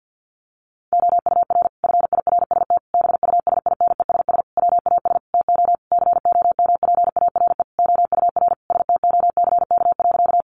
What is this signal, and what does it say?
Morse code, heard as a keyed tone, about 50 wpm, 700 Hz. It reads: OUR FIRST BUSINESS WAS TO COMPARE OUR STOCK.
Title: Sentences from Top 1300 Words 50wpm